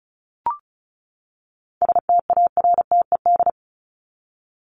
Morse code, heard as a keyed tone, about 34 wpm, 700 Hz. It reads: STARTED
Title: ED Suffix - Top 100 Words with Base Words 35wpm